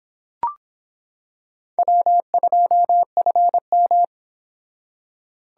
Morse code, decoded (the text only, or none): W2FM